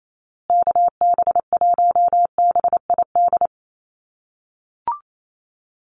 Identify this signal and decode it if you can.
Morse code, heard as a keyed tone, about 28 wpm, 700 Hz. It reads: KB1BID
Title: USA Call Signs Repeated 2x 28wpm